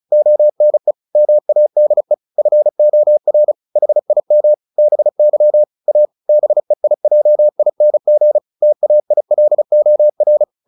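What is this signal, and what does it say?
Morse code, heard as a keyed tone, about 34 wpm, 600 Hz. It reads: ONE MADE FOR HIM BY A BEIJING TAILOR
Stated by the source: Morse practice at 35 wpm from CW med Gnister, sending Ken Follett's novel Never in English.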